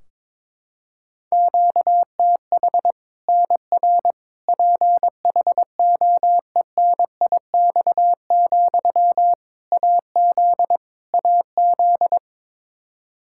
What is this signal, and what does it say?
Morse code, heard as a keyed tone, about 22 wpm, 700 Hz. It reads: QTH NR PHOENIX, AZ AZ
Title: K1USN SST QSO Practice 22wpm